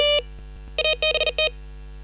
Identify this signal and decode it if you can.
Morse code, heard as a keyed tone, about 40 wpm, 2500 Hz. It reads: ABT
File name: software / Morserino-32 / 40-WPM-VHSC